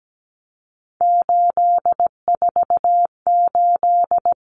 Morse code, heard as a keyed tone, about 17 wpm, 700 Hz. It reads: 848
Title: Common Cut Numbers - 3-Digit 17wpm